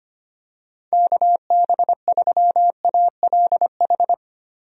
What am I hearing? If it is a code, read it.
Morse code, heard as a keyed tone, about 25 wpm, 700 Hz. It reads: KB3ALH
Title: USA Call Signs - 6 Characters - Repeated 3x - 25wpm